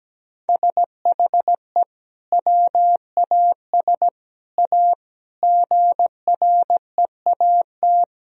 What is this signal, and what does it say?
Morse code, heard as a keyed tone, about 17 wpm, 700 Hz. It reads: SHE WAS A GREAT
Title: Sentences from Top 1000 Words - Review Part 4 - 17wpm